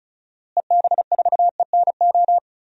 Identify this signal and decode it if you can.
Morse code, heard as a keyed tone, about 34 wpm, 700 Hz. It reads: EB4ENO
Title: DXCC-100 Call Signs - 6 Characters - Repeated 2x - 35wpm